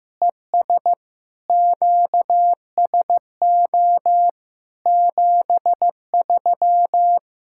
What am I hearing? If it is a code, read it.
Morse code, heard as a keyed tone, about 15 wpm, 700 Hz. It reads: ES QSO 73
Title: Practice QSO - Exchange 4 - Ending - 15wpm